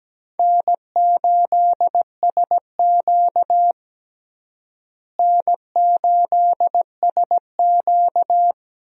Morse code, heard as a keyed tone, about 17 wpm, 700 Hz. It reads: N8SQ N8SQ
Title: USA Call Signs - 4 Characters - Repeated 2x - 17wpm